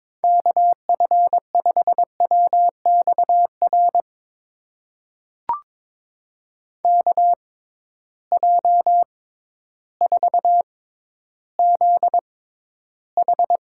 Morse code, heard as a keyed tone, about 22 wpm, spaced as for 12 wpm, 700 Hz. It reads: KF5WXR KJ4ZH
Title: USA Call Signs - 6 Characters - 6x Character Spacing - 22wpm